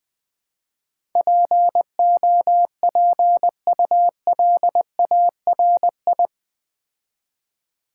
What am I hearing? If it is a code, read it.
Morse code, heard as a keyed tone, about 20 wpm, 700 Hz. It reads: POPULARI
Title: Build A Word - 4 Letters At A Time - 10-Letter Words - 20wpm